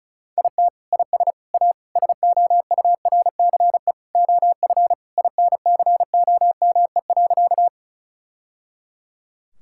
Morse code, heard as a keyed tone, about 34 wpm, 700 Hz. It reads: IT IS A SOURCE OF INCOME.
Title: Sentences from Top 1000 Words - Review Part 5 - 35wpm